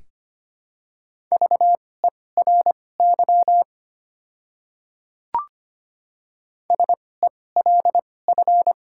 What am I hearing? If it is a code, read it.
Morse code, heard as a keyed tone, about 25 wpm, spaced as for 18 wpm, 700 Hz. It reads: VERY SELF
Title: 2x Character Spacing - Top 200 Words - 25wpm